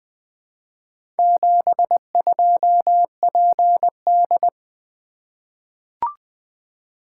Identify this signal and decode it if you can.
Morse code, heard as a keyed tone, about 20 wpm, 700 Hz. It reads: 72PD
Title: Four Letter-Number Combinations - 20wpm